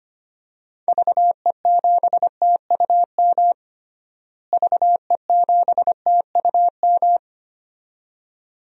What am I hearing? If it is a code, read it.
Morse code, heard as a keyed tone, about 25 wpm, 700 Hz. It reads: VE7TUM VE7TUM
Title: DXCC-100 Call Signs - 6 Characters - Repeated 2x - 25wpm